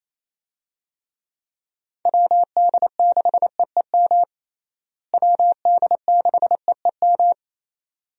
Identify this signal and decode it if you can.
Morse code, heard as a keyed tone, about 28 wpm, 700 Hz. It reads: WD6EEM WD6EEM
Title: USA Call Signs - 6 Characters - Repeated 2x - 28wpm